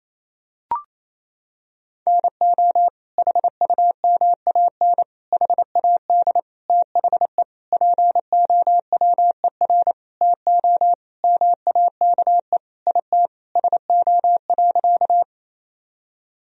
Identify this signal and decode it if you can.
Morse code, heard as a keyed tone, about 27 wpm, 700 Hz. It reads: NO HUMAN HAD THE POWER TO MAKE IT SO.
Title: Sentences from Top 1000 Words - Review Part 3 - 28wpm